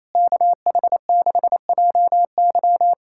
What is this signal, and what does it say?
Morse code, heard as a keyed tone, about 28 wpm, 700 Hz. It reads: KH6JY
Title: USA Callsign DE Callsign - 5 Characters - 28wpm